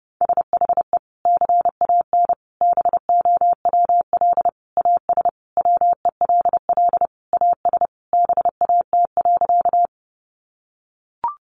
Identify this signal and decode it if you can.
Morse code, heard as a keyed tone, about 30 wpm, 700 Hz. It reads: SHE CAN BOWL AS WELL AS BAT.
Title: Sentences from Top 1500 Words 30wpm